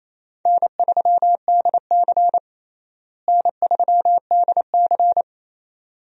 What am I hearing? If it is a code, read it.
Morse code, heard as a keyed tone, about 28 wpm, 700 Hz. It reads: N3DC N3DC
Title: USA Call Signs - 4 Characters - Repeated 2x - 28wpm